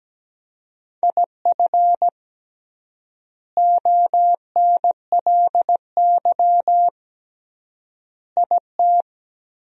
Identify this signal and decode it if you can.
Morse code, heard as a keyed tone, about 17 wpm, 700 Hz. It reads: IF ONLY IT
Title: 3x Word-Spacing - Sentences from Top 500 Words - 17wpm